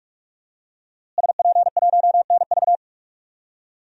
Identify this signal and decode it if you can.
Morse code, heard as a keyed tone, about 45 wpm, 700 Hz. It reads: SP1NV
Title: DXCC-100 Call Signs - 5 Characters - Repeated 3x - 45wpm